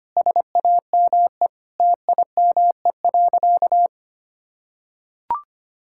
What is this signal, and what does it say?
Morse code, heard as a keyed tone, about 25 wpm, 700 Hz. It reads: SAME TIME.
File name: practice / 25wpm / Sentences with 3-Word Phrases - 25wpm